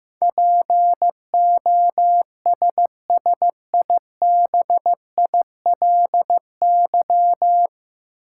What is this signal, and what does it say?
Morse code, heard as a keyed tone, about 15 wpm, 700 Hz. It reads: POSSIBILY
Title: Sentences from Top 1300 Words - New Word Drill - 15wpm